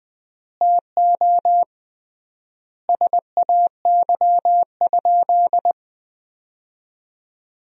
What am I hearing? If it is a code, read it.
Morse code, heard as a keyed tone, about 20 wpm, 700 Hz. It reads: TO SAY?
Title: 3x Word-Spacing - Sentences from Top 200 Words - 20wpm